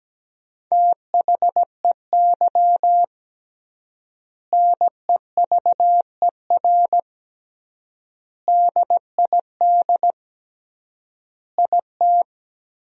Morse code, heard as a keyed tone, about 17 wpm, 700 Hz. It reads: THEY NEVER DID IT
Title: 3x Word-Spacing - Sentences from Top 300 Words - 17wpm